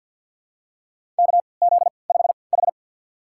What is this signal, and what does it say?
Morse code, heard as a keyed tone, about 47 wpm, spaced as for 28 wpm, 700 Hz. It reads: KZ5H
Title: USA Call Signs - 4 Characters - 3x Character Spacing - 50wpm